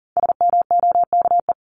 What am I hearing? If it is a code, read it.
Morse code, heard as a keyed tone, about 40 wpm, 700 Hz. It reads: SMOKE
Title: Top 1501-1750 Words 40wpm